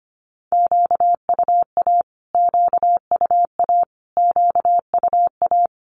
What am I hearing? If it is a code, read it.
Morse code, heard as a keyed tone, about 25 wpm, 700 Hz. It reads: QUA QUA QUA
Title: Q-Codes Repeated 3x 25wpm